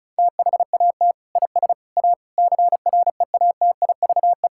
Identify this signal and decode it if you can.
Morse code, heard as a keyed tone, about 36 wpm, 700 Hz. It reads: THAT IS A CREATIVE
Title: Sentences from Top 1000 Words - Review Part 5 - 35wpm